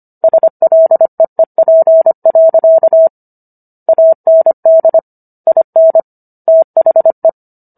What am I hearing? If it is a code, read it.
Morse code, heard as a keyed tone, about 25 wpm, 650 Hz. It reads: SLEEP. AND IN THE
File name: Book One / 16-Chapter 16 - 25 WPM